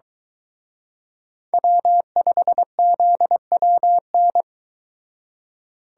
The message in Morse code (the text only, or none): W5ZWN